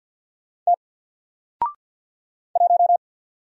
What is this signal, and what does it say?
Morse code, heard as a keyed tone, about 48 wpm, 700 Hz. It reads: T 1
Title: Single Letter-Number 50wpm